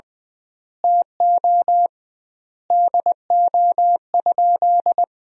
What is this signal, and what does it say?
Morse code, heard as a keyed tone, about 20 wpm, 700 Hz. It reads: TO DO?
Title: 2x Word-Spacing - Sentences from Top 100 Words - 20wpm